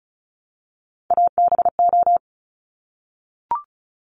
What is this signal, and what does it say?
Morse code, heard as a keyed tone, about 36 wpm, 700 Hz. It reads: ABO
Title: Top Three Letter Combinations in Words 35wpm